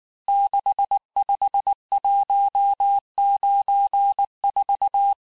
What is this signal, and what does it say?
Morse code, heard as a keyed tone, about 19 wpm, 800 Hz. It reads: 65194
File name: cours-du-lundi files / cours-1595-teste-CHIFFRE-vitesse18-MPM